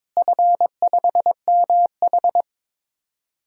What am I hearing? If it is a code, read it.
Morse code, heard as a keyed tone, about 22 wpm, 700 Hz. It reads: F5MH
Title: DXCC-100 Call Signs - 4 Characters - Repeated 3x - 22wpm